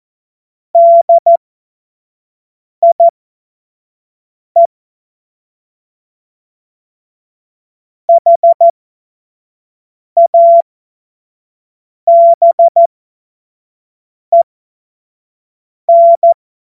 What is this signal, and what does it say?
Morse code, heard as a keyed tone, about 14 wpm, spaced as for 5 wpm, 680 Hz. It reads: DIE HABEN
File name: Morsekurs / Witze / Witz 07. Dezember 2021, 25 F70